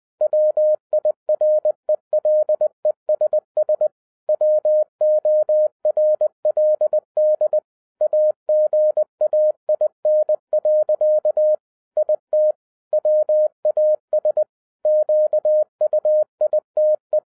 Morse code, heard as a keyed tone, about 20 wpm, 610 Hz. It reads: WIRELESS WORLD AGAIN. IT WAS QUITE
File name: 2012-01-mp3 / n05-CAME BACK TO HAM RADIO AFTER 30 YEARS OF INTERVAL-20wpm